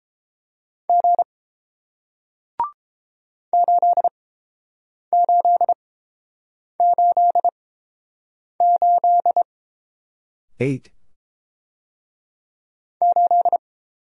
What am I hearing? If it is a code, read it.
Morse code, heard as a keyed tone, about 30 wpm, spaced as for 7 wpm, 700 Hz. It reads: G 8888 8